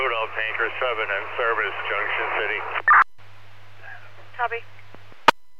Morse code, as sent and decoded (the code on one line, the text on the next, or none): none